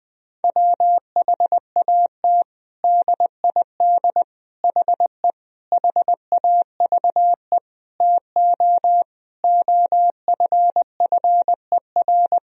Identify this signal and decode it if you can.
Morse code, heard as a keyed tone, about 19 wpm, 700 Hz. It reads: WHAT DID HE HAVE TO OFFER
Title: Sentences from Top 1000 Words - Review Part 4 - 20wpm